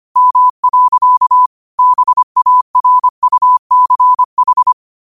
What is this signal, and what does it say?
Morse code, heard as a keyed tone, about 25 wpm, 1000 Hz. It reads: M. BARUCH
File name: qotdfiles / 25wpm